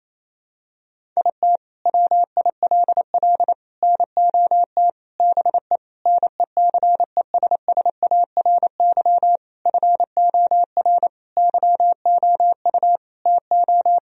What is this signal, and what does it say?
Morse code, heard as a keyed tone, about 28 wpm, 700 Hz. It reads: IT WILL NOT BE NECESSARY FOR YOU TO
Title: Sentences from Top 1000 Words - Review Part 1 - 28wpm